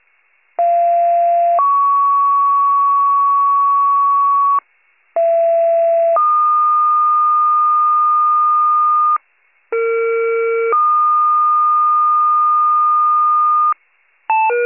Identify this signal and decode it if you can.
Two-tone page, heard as a one-second tone, then a three-second tone, about 690 then 1120 Hz, about 670 then 1220 Hz, about 460 then 1220 Hz. It is heard from 0.6 to 13.7 s.
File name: pager / 41 RESCUE / wcalert